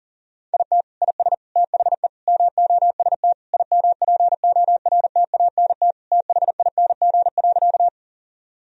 Morse code, heard as a keyed tone, about 40 wpm, 700 Hz. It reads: IT IS THE MOST IMPORTANT THING.
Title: Sentences from Top 1000 Words - Review Part 1 - 40wpm